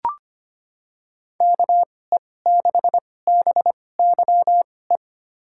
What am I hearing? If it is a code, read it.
Morse code, heard as a keyed tone, about 25 wpm, spaced as for 18 wpm, 700 Hz. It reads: KE6BYE
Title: USA Call Signs - 6 Characters - 2x Character Spacing - 25wpm